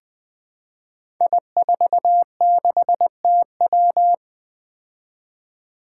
Morse code, heard as a keyed tone, about 20 wpm, 700 Hz. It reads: I46TW
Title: Five Challenging Letter-Number Combinations 20wpm